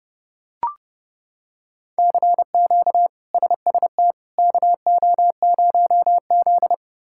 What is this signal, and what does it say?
Morse code, heard as a keyed tone, about 31 wpm, 700 Hz. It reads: CQ SST KO0Z
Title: K1USN SST QSO Practice - 30wpm